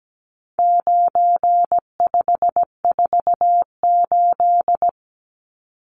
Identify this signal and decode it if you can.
Morse code, heard as a keyed tone, about 17 wpm, 700 Hz. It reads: 9548
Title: Numbers - 4-Digit 17wpm